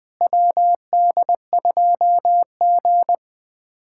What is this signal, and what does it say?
Morse code, heard as a keyed tone, about 21 wpm, 700 Hz. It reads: WD2G
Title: USA Callsign DE Callsign - 4 Characters - 20wpm